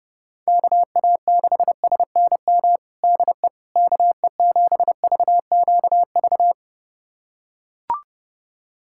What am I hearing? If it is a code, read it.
Morse code, heard as a keyed tone, about 32 wpm, 700 Hz. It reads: KA6SNM DE KE7VQV
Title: USA Callsign DE Callsign - 6 Characters - 30wpm